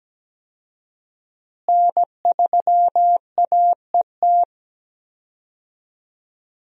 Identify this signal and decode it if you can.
Morse code, heard as a keyed tone, about 17 wpm, 700 Hz. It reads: N3AET
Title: USA Call Signs - 5 Characters - Repeated 2x - 17wpm